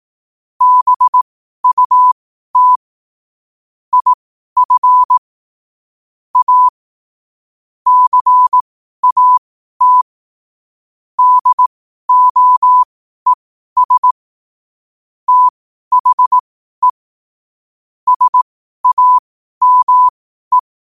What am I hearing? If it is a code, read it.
Morse code, heard as a keyed tone, about 18 wpm, spaced as for 12 wpm, 1000 Hz. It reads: BUT IF A CAT DOES THE SAME
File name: qotdfiles / 12wpm